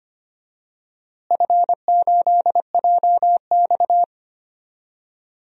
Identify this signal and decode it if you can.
Morse code, heard as a keyed tone, about 25 wpm, 700 Hz. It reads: F8JX
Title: DXCC-100 Call Signs - 4 Characters - Repeated 3x - 25wpm